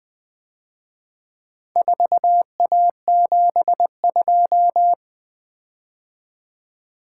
Morse code, heard as a keyed tone, about 24 wpm, 700 Hz. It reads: 4A72
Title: Speed-Racing - Four Letter-Number Combinations - 20wpm